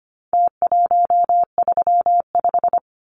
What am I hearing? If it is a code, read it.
Morse code, heard as a keyed tone, about 25 wpm, 700 Hz. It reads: T135
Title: Numbers - Contest Style - 4-Digit 25wpm